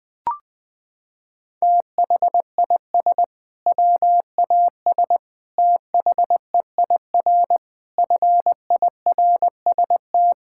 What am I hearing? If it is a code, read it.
Morse code, heard as a keyed tone, about 21 wpm, 700 Hz. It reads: THIS WAS THEIR FIRST
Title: Sentences from Top 1600 Words 20wpm